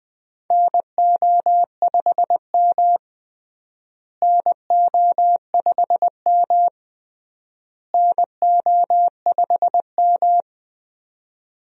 Morse code, heard as a keyed tone, about 20 wpm, 700 Hz. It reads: NO5M NO5M NO5M
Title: USA Call Signs - 4 Characters - Repeated 3x - 20wpm